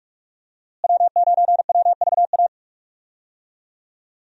Morse code, heard as a keyed tone, about 44 wpm, 700 Hz. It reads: W9WVA
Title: USA Call Signs - 5 Characters - Repeated 2x - 45wpm